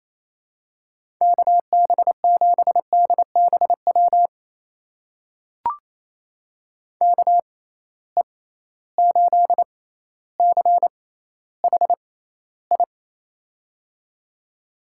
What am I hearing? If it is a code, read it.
Morse code, heard as a keyed tone, about 28 wpm, spaced as for 14 wpm, 700 Hz. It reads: KB7DBW KE8CHI
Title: USA Call Signs - 6 Characters - 6x Character Spacing - 28wpm